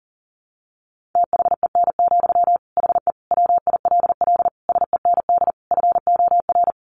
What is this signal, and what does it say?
Morse code, heard as a keyed tone, about 40 wpm, 700 Hz. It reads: THEN, HE WILL SEND FOR
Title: Sentences from Top 1500 Words 40wpm